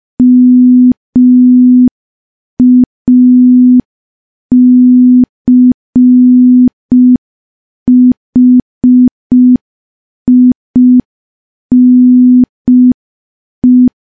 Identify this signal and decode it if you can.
Morse code, heard as a keyed tone, about 5 wpm, 250 Hz. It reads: MACHINE
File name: morse / morse-training-05WPM